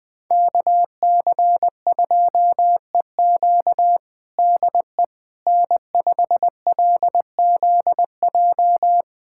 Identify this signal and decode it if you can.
Morse code, heard as a keyed tone, about 20 wpm, 700 Hz. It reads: KC2EQ DE N5LZJ
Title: USA Callsign DE Callsign - 5 Characters - 20wpm